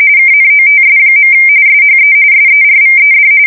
RTTY, about 45.45 baud. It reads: KNOTS. SHE SERVED
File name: fsk170 / battleshipnj